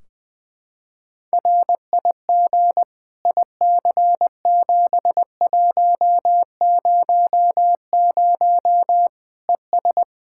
Morse code, heard as a keyed tone, about 21 wpm, 700 Hz. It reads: RIG IC7100 ES
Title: K1USN SST QSO Practice 20wpm